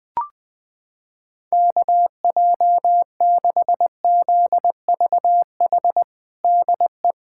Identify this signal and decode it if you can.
Morse code, heard as a keyed tone, about 21 wpm, 700 Hz. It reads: KJ6ZVH DE
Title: USA Callsign DE Callsign 20wpm